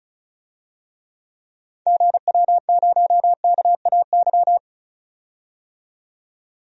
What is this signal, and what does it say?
Morse code, heard as a keyed tone, about 36 wpm, 700 Hz. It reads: GW0KAY
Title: DXCC-100 Call Signs - 6 Characters - 35wpm